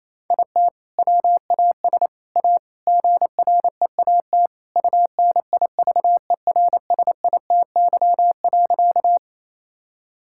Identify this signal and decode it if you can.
Morse code, heard as a keyed tone, about 28 wpm, 700 Hz. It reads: IT WAS A GREAT UNIVERSITY.
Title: Sentences from Top 1000 Words - Review Part 5 - 28wpm